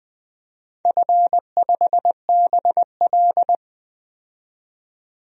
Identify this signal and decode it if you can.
Morse code, heard as a keyed tone, about 20 wpm, 700 Hz. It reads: F5BL
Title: DXCC-100 Call Signs - 4 Characters - 20wpm